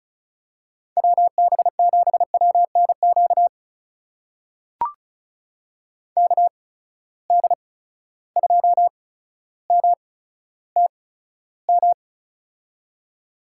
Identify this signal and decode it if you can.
Morse code, heard as a keyed tone, about 36 wpm, spaced as for 15 wpm, 700 Hz. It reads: WB7WNQ KD2MTM
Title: USA Call Signs - 6 Characters - 8x Character Spacing - 35wpm